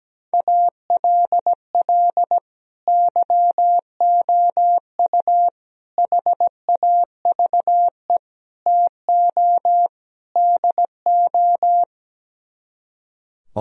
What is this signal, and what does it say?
Morse code, heard as a keyed tone, about 17 wpm, 700 Hz. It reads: ALL YOU HAVE TO DO
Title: Speed-Racing - Sets of 5 Words 17wpm